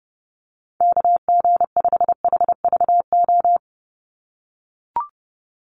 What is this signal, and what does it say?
Morse code, heard as a keyed tone, about 30 wpm, 700 Hz. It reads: KG5HVO
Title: Call Signs - 30wpm